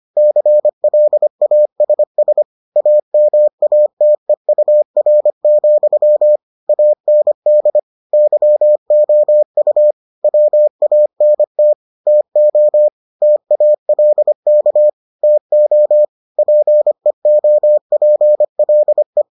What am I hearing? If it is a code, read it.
Morse code, heard as a keyed tone, about 25 wpm, 600 Hz. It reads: CLASS AMATEUR, AND YOU WANT TO TALK TO PEOPLE